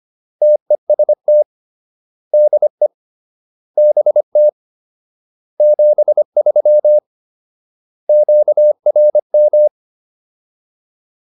Morse code, heard as a keyed tone, about 25 wpm, 600 Hz. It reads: TEST DE BT 73 QRM